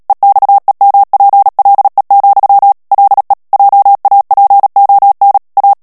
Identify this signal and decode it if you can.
Morse code, heard as a keyed tone, about 37 wpm, 800 Hz. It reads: EXEMPLE, LE JAPON A
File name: CW / FICHIERS-MP3 / FRANCAIS TEXTE CLAIR 1 VITESSE 35 MOTS MINUTES